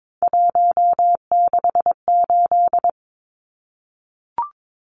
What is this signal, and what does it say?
Morse code, heard as a keyed tone, about 22 wpm, 700 Hz. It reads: 168
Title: Numbers - 3-Digit 22wpm